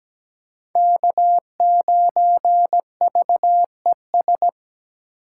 Morse code, heard as a keyed tone, about 17 wpm, 700 Hz. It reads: K9VES
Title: USA Call Signs - 5 Characters - 3x Character Spacing - 17wpm